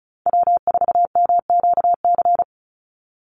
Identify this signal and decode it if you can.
Morse code, heard as a keyed tone, about 34 wpm, 700 Hz. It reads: W4MQC
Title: Call Signs - 35wpm